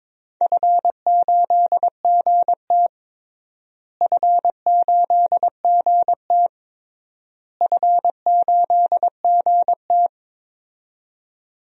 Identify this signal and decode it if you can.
Morse code, heard as a keyed tone, about 22 wpm, 700 Hz. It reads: F8GT F8GT F8GT
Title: DXCC-100 Call Signs - 4 Characters - Repeated 3x - 22wpm